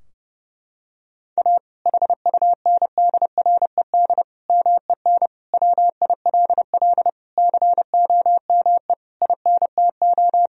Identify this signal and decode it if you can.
Morse code, heard as a keyed tone, about 30 wpm, 700 Hz. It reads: A HUNDRED MEN WILL COME INTO
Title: Sentences from Top 1000 Words - Review Part 2 - 30wpm